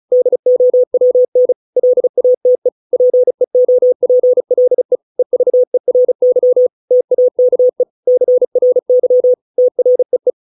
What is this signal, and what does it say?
Morse code, heard as a keyed tone, about 34 wpm, 500 Hz. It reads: DOWN LATE PEOPLE EVERY TAKE CRY TREE